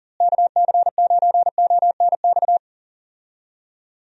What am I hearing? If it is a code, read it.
Morse code, heard as a keyed tone, about 40 wpm, 700 Hz. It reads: KC9ONX
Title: USA Call Signs - 6 Characters - Repeated 3x - 40wpm